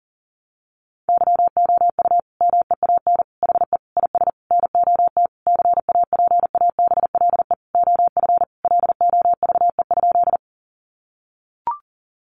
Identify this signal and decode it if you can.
Morse code, heard as a keyed tone, about 37 wpm, 700 Hz. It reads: YOU MEAN HE IS NOT CAPABLE OF LOVE?
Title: Sentences from Top 1200 Words 40wpm